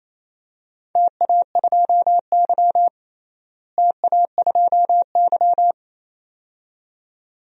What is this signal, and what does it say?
Morse code, heard as a keyed tone, about 28 wpm, 700 Hz. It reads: TA2Y TA2Y
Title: DXCC-100 Call Signs - 4 Characters - Repeated 2x - 28wpm